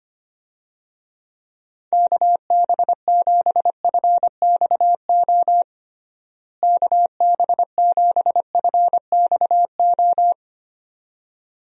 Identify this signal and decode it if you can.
Morse code, heard as a keyed tone, about 25 wpm, 700 Hz. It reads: KB7FXO KB7FXO
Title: USA Call Signs - 6 Characters - Repeated 2x - 25wpm